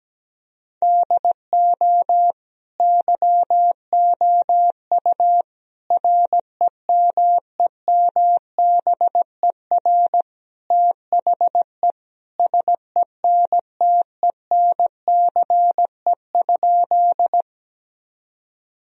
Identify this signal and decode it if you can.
Morse code, heard as a keyed tone, about 17 wpm, 700 Hz. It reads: DO YOU REMEMBER THE SENTENCE?
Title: Sentences from Top 1000 Words - Review Part 1 - 17wpm